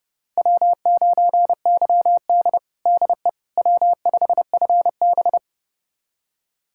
Morse code, heard as a keyed tone, about 30 wpm, 700 Hz. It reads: W9YD DE W5FB
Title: USA Callsign DE Callsign - 4 Characters - 30wpm